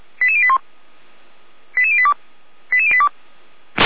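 Five-tone selective call (ZVEI), heard as quick runs of five tones, tones of about 70 ms. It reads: 88081 88082 88082
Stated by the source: Five-tone ZVEI selective call received on an SDR.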